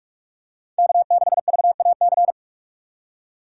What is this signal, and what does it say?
Morse code, heard as a keyed tone, about 46 wpm, 700 Hz. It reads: K6VAC
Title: USA Call Signs - 2x Character Spacing - 45wpm